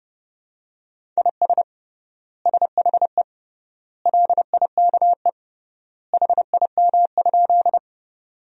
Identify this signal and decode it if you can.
Morse code, heard as a keyed tone, about 31 wpm, 700 Hz. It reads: IS SHE LIKE HIM?
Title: 3x Word-Spacing - Sentences from Top 100 Words - 30wpm